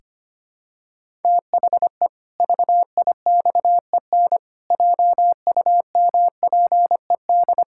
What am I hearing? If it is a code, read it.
Morse code, heard as a keyed tone, about 25 wpm, 700 Hz. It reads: THE VIXEN JUMPED
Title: Pangram Challenge v2 - 25wpm